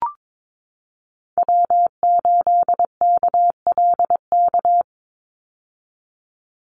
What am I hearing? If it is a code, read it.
Morse code, heard as a keyed tone, about 22 wpm, 700 Hz. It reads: W8KLK
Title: USA Call Signs 22wpm